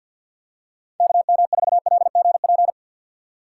Morse code, heard as a keyed tone, about 50 wpm, 700 Hz. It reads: KM4LGP
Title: USA Call Signs - 6 Characters - 3x Character Spacing - 50wpm